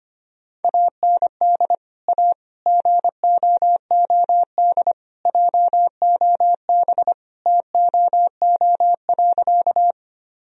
Morse code, heard as a keyed tone, about 25 wpm, 700 Hz. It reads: AND A GOOD JOB TOO.
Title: Sentences from Top 1000 Words - Review Part 3 - 25wpm